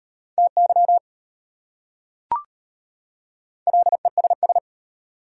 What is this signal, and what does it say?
Morse code, heard as a keyed tone, about 31 wpm, 700 Hz. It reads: TY LESS
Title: Speed-Racing Prefixes and Suffixes 25wpm